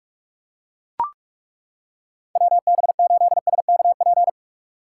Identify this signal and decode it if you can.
Morse code, heard as a keyed tone, about 42 wpm, 700 Hz. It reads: WB8SKP
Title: K1USN SST QSO Practice - 45wpm